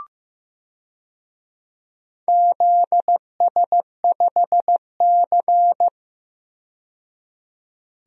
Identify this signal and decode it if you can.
Morse code, heard as a keyed tone, about 15 wpm, 700 Hz. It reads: ZS5C
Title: DXCC-100 Call Signs - 4 Characters - 15wpm